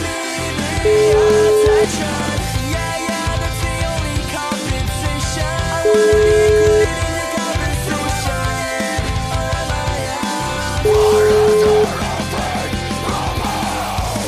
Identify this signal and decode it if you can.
Ringback tone (Europe, 425 Hz), with noise.